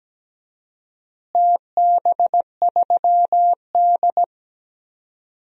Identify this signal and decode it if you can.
Morse code, heard as a keyed tone, about 17 wpm, 700 Hz. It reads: TB3D
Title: DXCC-100 Call Signs - 4 Characters - Repeated 2x - 17wpm